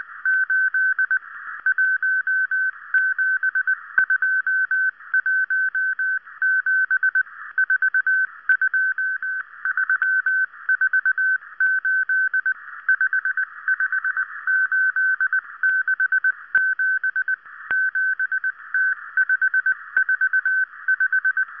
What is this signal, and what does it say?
Morse code, heard as a keyed tone, about 19 wpm, 1550 Hz. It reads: 8 17217 42348 55867 7T545